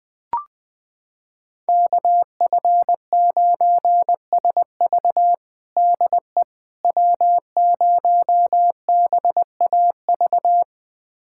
Morse code, heard as a keyed tone, about 20 wpm, 700 Hz. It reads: KF9SV DE W0BAV
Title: USA Callsign DE Callsign - 5 Characters - 20wpm